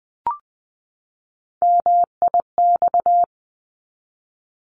Morse code, heard as a keyed tone, about 20 wpm, 700 Hz. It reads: MIX
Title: 260 Lesson 42e - Words 20-20wpm